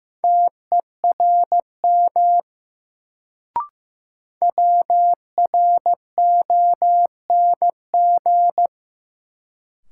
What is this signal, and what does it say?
Morse code, heard as a keyed tone, about 15 wpm, 700 Hz. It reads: TERM WRONG
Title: Sentences from Top 900 Words - New Word Drill - 15wpm